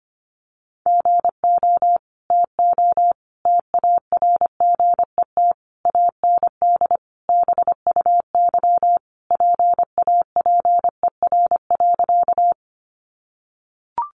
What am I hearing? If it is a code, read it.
Morse code, heard as a keyed tone, about 25 wpm, 700 Hz. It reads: GO TO TARGET AND BUY PAPER.
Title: Sentences from Top 1200 Words 25wpm